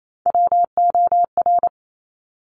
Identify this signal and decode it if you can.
Morse code, heard as a keyed tone, about 28 wpm, 700 Hz. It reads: WOR
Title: Top Three Letter Combinations in Words 28wpm